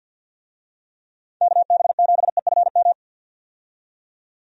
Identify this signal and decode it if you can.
Morse code, heard as a keyed tone, about 52 wpm, 700 Hz. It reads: KB7EFM
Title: USA Call Signs - 6 Characters - 50wpm